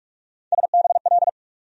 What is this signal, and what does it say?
Morse code, heard as a keyed tone, about 46 wpm, 700 Hz. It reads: SBL
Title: USA Call Sign Postfixes - 3 Letters 45wpm